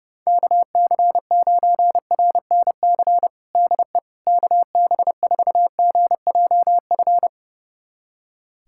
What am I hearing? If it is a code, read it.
Morse code, heard as a keyed tone, about 30 wpm, 700 Hz. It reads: KC9RNC DE KB4GJF
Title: USA Callsign DE Callsign 30wpm